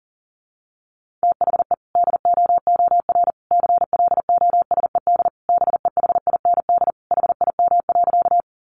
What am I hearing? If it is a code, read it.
Morse code, heard as a keyed tone, about 40 wpm, 700 Hz. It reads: THE DOOR CLOSED BEHIND HIM.
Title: Sentences from Top 1400 Words 40wpm